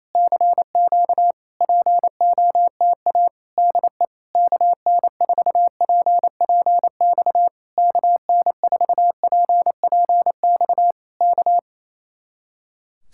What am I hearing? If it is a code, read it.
Morse code, heard as a keyed tone, about 26 wpm, 700 Hz. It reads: CQ POTA DE KN4PPX KN4PPX K
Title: Speed-Racing Common QSO Phrases Encore - 28wpm